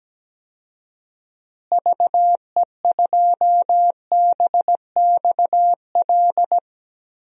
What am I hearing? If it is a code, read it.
Morse code, heard as a keyed tone, about 17 wpm, 700 Hz. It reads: VE2BXL
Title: DXCC-100 Call Signs - 6 Characters - Repeated 2x - 17wpm